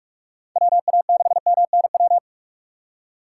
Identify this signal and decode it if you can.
Morse code, heard as a keyed tone, about 42 wpm, 700 Hz. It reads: WA6MNW
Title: USA Call Signs - 6 Characters - 2x Character Spacing - 45wpm